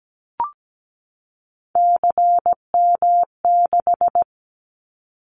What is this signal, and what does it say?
Morse code, heard as a keyed tone, about 17 wpm, 700 Hz. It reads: CM6
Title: Callsign Prefixes for Top 100 DXCC - 17wpm